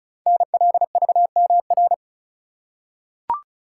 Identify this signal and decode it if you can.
Morse code, heard as a keyed tone, about 36 wpm, 700 Hz. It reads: NLVMR
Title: Code Groups - 5 Letters - 35wpm